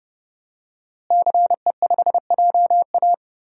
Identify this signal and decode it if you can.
Morse code, heard as a keyed tone, about 30 wpm, 700 Hz. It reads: CE5JA
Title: DXCC-100 Call Signs - 5 Characters - Repeated 3x - 30wpm